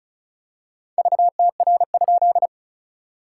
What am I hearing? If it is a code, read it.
Morse code, heard as a keyed tone, about 34 wpm, 690 Hz. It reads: VTR?
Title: SOTA QSO Practice 35wpm